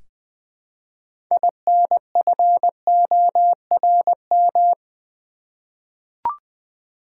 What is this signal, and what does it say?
Morse code, heard as a keyed tone, about 21 wpm, 700 Hz. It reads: INFORM
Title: Six Letter Words - 20wpm